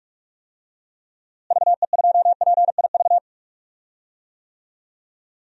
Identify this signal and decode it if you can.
Morse code, heard as a keyed tone, about 41 wpm, 700 Hz. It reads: VE2PIV
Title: DXCC-100 Call Signs - 6 Characters - Repeated 3x - 45wpm